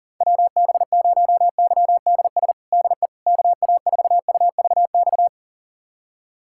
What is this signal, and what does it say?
Morse code, heard as a keyed tone, about 37 wpm, 700 Hz. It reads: WB0YDS DE KA4UVX
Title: USA Callsign DE Callsign - 6 Characters - 40wpm